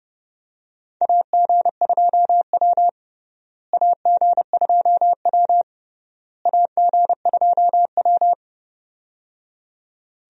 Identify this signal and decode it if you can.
Morse code, heard as a keyed tone, about 31 wpm, 700 Hz. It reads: AG2W AG2W AG2W
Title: USA Call Signs - 4 Characters - Repeated 3x - 30wpm